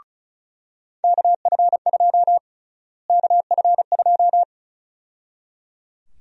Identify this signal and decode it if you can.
Morse code, heard as a keyed tone, about 34 wpm, 700 Hz. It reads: KF2 KF2
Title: USA Call Sign Prefixes - Repeated 2x - 35wpm